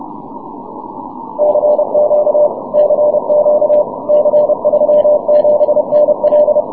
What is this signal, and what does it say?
Morse code, heard as a keyed tone, about 30 wpm, 600 Hz. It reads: CQ CQ C37NL
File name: wp / c37nl 6m cw